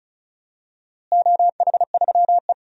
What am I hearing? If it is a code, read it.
Morse code, heard as a keyed tone, about 36 wpm, 690 Hz. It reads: OH3E
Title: DXCC-100 Call Signs - 4 Characters - Repeated 3x - 35wpm